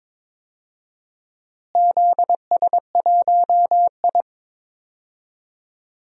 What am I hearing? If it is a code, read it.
Morse code, heard as a keyed tone, about 22 wpm, 700 Hz. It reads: ZS1I
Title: DXCC-100 Call Signs - 4 Characters - Repeated 2x - 22wpm